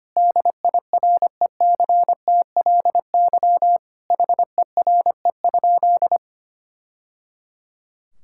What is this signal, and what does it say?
Morse code, heard as a keyed tone, about 25 wpm, 700 Hz. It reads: DIRECTLY HERE?
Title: Sentences from Top 1000 Words - Review Part 5 - 25wpm